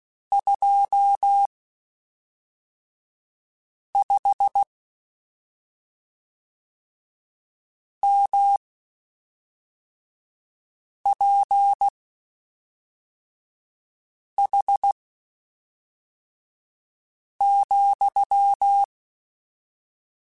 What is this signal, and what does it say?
Morse code, heard as a keyed tone, about 16 wpm, spaced as for 3 wpm, 800 Hz. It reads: 25 MPH,
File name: Spaced Characters / Spaced Characters-mp3s / Weather5 5 sp @5